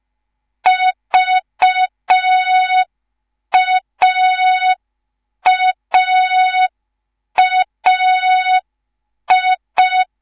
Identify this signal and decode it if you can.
Morse code, heard as a keyed tone, about 5 wpm, 3700 Hz. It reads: VAAAI